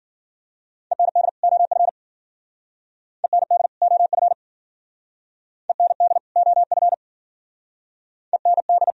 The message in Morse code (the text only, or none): END OF END OF END OF END